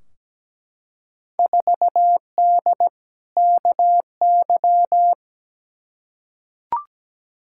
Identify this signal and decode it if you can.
Morse code, heard as a keyed tone, about 17 wpm, 700 Hz. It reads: 4D KY
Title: Field Day Exchanges Encore - 17wpm